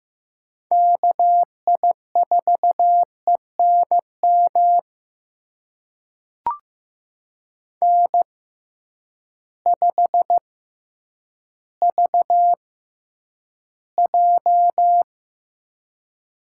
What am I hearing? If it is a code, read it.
Morse code, heard as a keyed tone, about 15 wpm, spaced as for 9 wpm, 700 Hz. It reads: KI4ENM N5VJ
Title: USA Call Signs - 6x Character Spacing - 15wpm